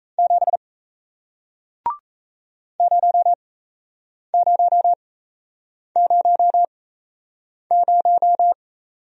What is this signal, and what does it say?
Morse code, heard as a keyed tone, about 37 wpm, spaced as for 8 wpm, 700 Hz. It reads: 7 0000